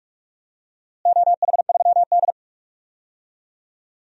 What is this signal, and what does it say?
Morse code, heard as a keyed tone, about 43 wpm, 700 Hz. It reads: OH3D
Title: DXCC-100 Call Signs - 4 Characters - Repeated 2x - 45wpm